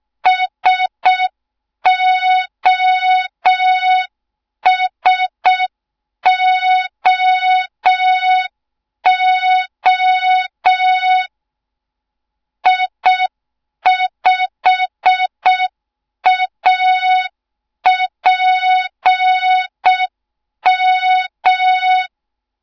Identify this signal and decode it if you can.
Morse code, heard as a keyed tone, about 6 wpm, 3700 Hz. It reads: SOSOO I5APM
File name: Morze-HA2MN / hang / 07 GYAKORLAT